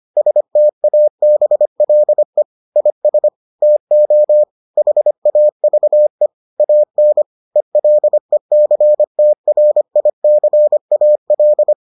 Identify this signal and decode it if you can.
Morse code, heard as a keyed tone, about 25 wpm, 600 Hz. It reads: STABLE IS TO HAVE AN ELECTRICAL